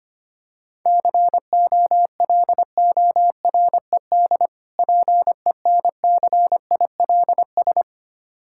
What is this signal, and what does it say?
Morse code, heard as a keyed tone, about 25 wpm, 700 Hz. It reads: COLORED PENCILS
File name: practice / 25wpm / Difficult Words with Hints - 25wpm